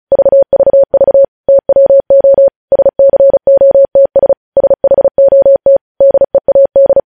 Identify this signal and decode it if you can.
Morse code, heard as a keyed tone, about 36 wpm, 550 Hz. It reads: VVV TWO SCOTS SHOT DEAD
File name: podcast / News headlines-1x-35x35